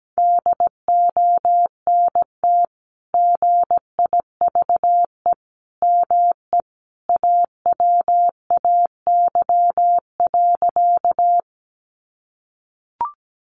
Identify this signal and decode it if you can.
Morse code, heard as a keyed tone, about 17 wpm, 700 Hz. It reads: DONT GIVE ME AWAY.
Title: Sentences from Top 600 Words 17wpm